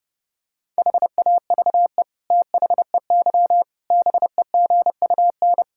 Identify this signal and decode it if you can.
Morse code, heard as a keyed tone, about 30 wpm, 700 Hz. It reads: HAVE THEY BEGUN
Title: Sentences from Top 1000 Words - Review Part 6 - 30wpm